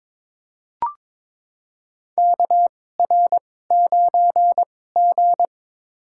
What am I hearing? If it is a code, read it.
Morse code, heard as a keyed tone, about 22 wpm, spaced as for 16 wpm, 700 Hz. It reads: KR9G
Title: USA Call Signs - 4 Characters - 2x Character Spacing - 22wpm